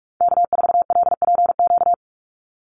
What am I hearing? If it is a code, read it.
Morse code, heard as a keyed tone, about 44 wpm, 700 Hz. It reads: K4LPQ
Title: NA SOTA Activator Callsigns 45wpm